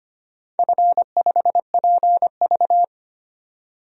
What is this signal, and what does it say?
Morse code, heard as a keyed tone, about 25 wpm, 700 Hz. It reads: F5PV
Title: DXCC-100 Call Signs - 4 Characters - 25wpm